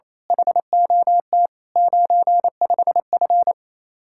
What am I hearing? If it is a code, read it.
Morse code, heard as a keyed tone, about 27 wpm, 700 Hz. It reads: HOT 95F
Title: Speed-Racing Common QSO Phrases - 28wpm